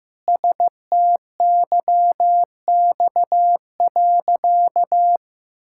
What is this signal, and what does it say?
Morse code, heard as a keyed tone, about 16 wpm, 700 Hz. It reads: STYX.
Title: Pangram Challenge v2 - 15wpm